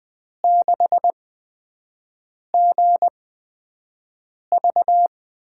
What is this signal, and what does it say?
Morse code, heard as a keyed tone, about 21 wpm, spaced as for 5 wpm, 700 Hz. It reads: 6GV
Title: USA Call Signs - 5 Characters - 8x Character Spacing - 20wpm